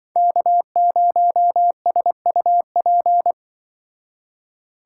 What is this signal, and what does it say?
Morse code, heard as a keyed tone, about 24 wpm, 700 Hz. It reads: K0SUP
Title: Speed Racing - USA 1x3 Call Signs 20wpm